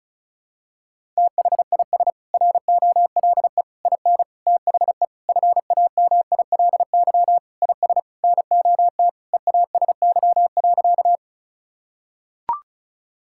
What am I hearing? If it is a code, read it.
Morse code, heard as a keyed tone, about 34 wpm, 700 Hz. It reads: THIS ROLE IN THE FAMILY IS NOT EASY.
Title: Sentences from Top 1000 Words - Review Part 6 - 35wpm